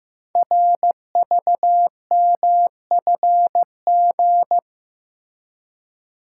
Morse code, heard as a keyed tone, about 15 wpm, 700 Hz. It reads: RVMFG
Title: Code Groups - 5 Letter:Numbers - 15wpm